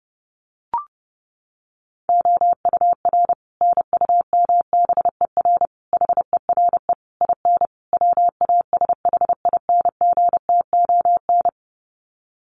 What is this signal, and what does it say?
Morse code, heard as a keyed tone, about 30 wpm, 700 Hz. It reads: OUR NUMBER HERE IN WASHINGTON
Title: Sets of 5 Words Encore2 30wpm